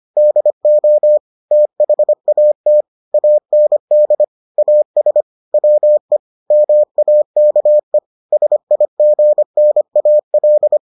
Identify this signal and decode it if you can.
Morse code, heard as a keyed tone, about 25 wpm, 600 Hz. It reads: DO THAT AND AS WE MAKE SIGNAL